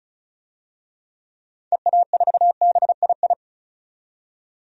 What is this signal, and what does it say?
Morse code, heard as a keyed tone, about 36 wpm, 700 Hz. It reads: EA4BII
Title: DXCC-100 Call Signs - 6 Characters - 35wpm